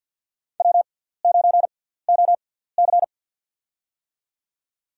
Morse code, heard as a keyed tone, about 52 wpm, spaced as for 17 wpm, 700 Hz. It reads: W9OC
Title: USA Call Signs - 4 Characters - 6x Character Spacing - 50wpm